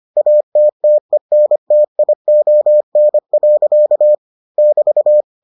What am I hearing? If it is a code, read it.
Morse code, heard as a keyed tone, about 25 wpm, 600 Hz. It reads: ATTENTION. <BT>